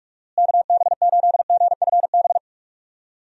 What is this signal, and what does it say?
Morse code, heard as a keyed tone, about 44 wpm, 700 Hz. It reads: KB8GFB
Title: USA Call Signs - 6 Characters - Repeated 2x - 45wpm